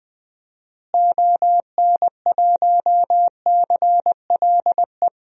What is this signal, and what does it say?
Morse code, heard as a keyed tone, about 20 wpm, 700 Hz. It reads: ON1CLE
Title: DXCC-100 Call Signs - 6 Characters - Repeated 2x - 20wpm